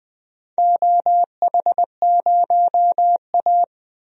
Morse code, heard as a keyed tone, about 20 wpm, 700 Hz. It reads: OH0A
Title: DXCC-100 Call Signs - 4 Characters - 20wpm